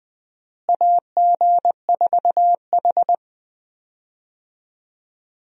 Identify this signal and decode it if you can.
Morse code, heard as a keyed tone, about 20 wpm, 700 Hz. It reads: AG4H
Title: USA Call Signs - 4 Characters - Repeated 2x - 20wpm